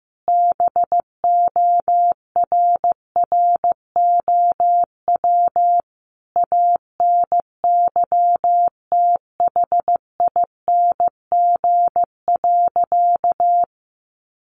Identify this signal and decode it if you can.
Morse code, heard as a keyed tone, about 15 wpm, 700 Hz. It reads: BORROW ANYTHING.
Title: Sentences from Top 1500 Words 15wpm